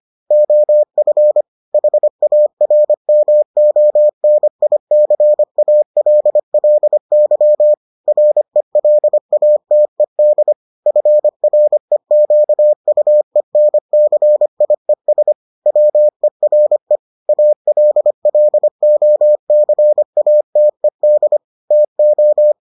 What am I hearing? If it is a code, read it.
Morse code, heard as a keyed tone, about 25 wpm, 600 Hz. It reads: OF HARMONICALLY RELATED FREQUENCIES WERE ALLOCATED TO